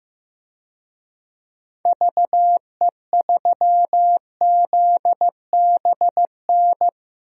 Morse code, heard as a keyed tone, about 15 wpm, 700 Hz. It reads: VE3ZBN
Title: DXCC-100 Call Signs - 6 Characters - Repeated 2x - 15wpm